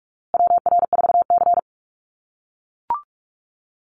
Morse code, heard as a keyed tone, about 44 wpm, 700 Hz. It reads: WR4C
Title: USA Call Signs Repeated 3x 45wpm